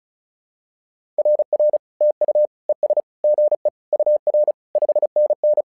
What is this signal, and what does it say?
Morse code, heard as a keyed tone, about 36 wpm, 600 Hz. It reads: RR TU ES GE UR 5NN
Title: SOTA QSO Practice 35wpm